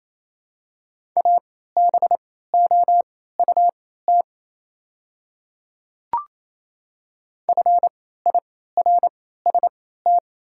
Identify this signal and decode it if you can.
Morse code, heard as a keyed tone, about 28 wpm, spaced as for 16 wpm, 700 Hz. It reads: ABOUT FIRST
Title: 3x Character Spacing - Top 100 Words - 28wpm